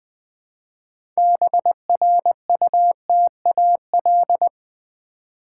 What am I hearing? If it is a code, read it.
Morse code, heard as a keyed tone, about 20 wpm, 690 Hz. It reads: BRUTAL
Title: Build A Word - 2 Letters At A Time - 6-Letter Words - 20wpm